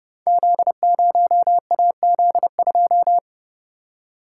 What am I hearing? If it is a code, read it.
Morse code, heard as a keyed tone, about 30 wpm, 700 Hz. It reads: Z0AZ2
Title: Code Groups - 5 Letter:Numbers - 30wpm